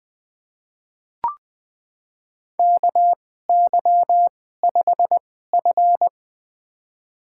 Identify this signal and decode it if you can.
Morse code, heard as a keyed tone, about 20 wpm, spaced as for 15 wpm, 700 Hz. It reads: KY5F
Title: USA Call Signs - 4 Characters - 2x Character Spacing - 20wpm